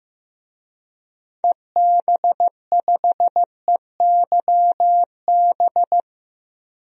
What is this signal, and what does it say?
Morse code, heard as a keyed tone, about 15 wpm, 700 Hz. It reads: EB5EYB
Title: DXCC-100 Call Signs - 6 Characters - Repeated 2x - 15wpm